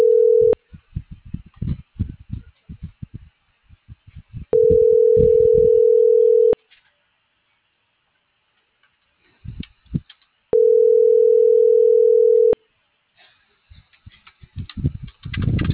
Ringback tone (North America), 440 + 480 Hz.